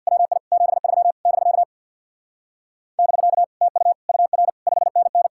Morse code, heard as a keyed tone, about 62 wpm, 700 Hz. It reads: WI 73 <BK> <BK> TU UR 5NN
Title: Speed-Racing Common QSO Phrases - 50wpm